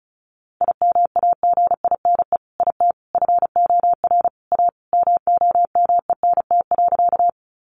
Morse code, heard as a keyed tone, about 34 wpm, 700 Hz. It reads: IMAGINE IT FOR A MOMENT.
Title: Sentences from Top 1400 Words 35wpm